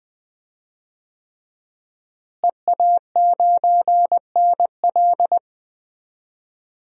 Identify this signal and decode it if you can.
Morse code, heard as a keyed tone, about 21 wpm, 700 Hz. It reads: EA9NL
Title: DXCC-100 Call Signs - 5 Characters - Repeated 3x - 20wpm